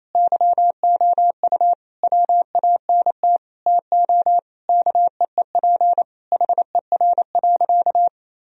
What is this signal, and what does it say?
Morse code, heard as a keyed tone, about 28 wpm, 700 Hz. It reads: YOU WANT TO KEEP HER.
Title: Sentences from Top 1000 Words - Review Part 2 - 28wpm